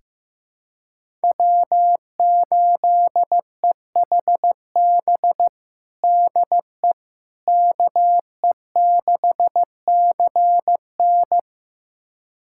Morse code, heard as a keyed tone, about 15 wpm, 700 Hz. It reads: W8EHB DE KE6CN
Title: USA Callsign DE Callsign - 5 Characters - 15wpm